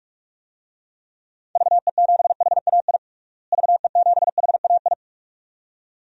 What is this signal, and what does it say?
Morse code, heard as a keyed tone, about 44 wpm, 700 Hz. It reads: VE7HAI VE7HAI
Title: DXCC-100 Call Signs - 6 Characters - Repeated 2x - 45wpm